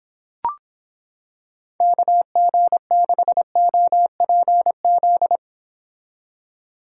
Morse code, heard as a keyed tone, about 26 wpm, 700 Hz. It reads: KG6OPZ